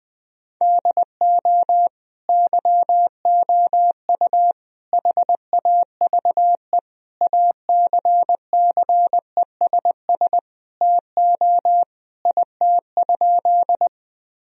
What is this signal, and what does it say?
Morse code, heard as a keyed tone, about 21 wpm, 700 Hz. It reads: DO YOU HAVE ACCESS TO IT?
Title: Sentences from Top 1000 Words - Review Part 4 - 20wpm